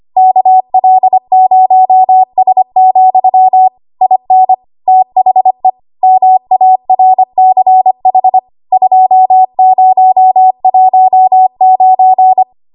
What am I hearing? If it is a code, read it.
Morse code, heard as a keyed tone, about 25 wpm, 750 Hz. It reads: KL0S, IN THE MARCH 2019
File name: Archive / 25 WPM / 25WPM